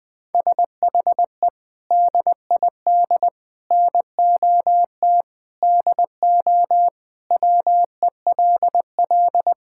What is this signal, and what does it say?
Morse code, heard as a keyed tone, about 20 wpm, 700 Hz. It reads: SHE DID NOT DO WELL
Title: Sentences from Top 1000 Words - Review Part 5 - 20wpm